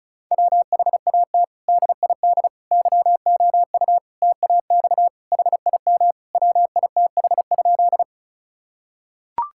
Morse code, heard as a keyed tone, about 36 wpm, 700 Hz. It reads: WHAT DID YOU TAX HIM WITH?
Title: Sentences from Top 1000 Words - Review Part 5 - 35wpm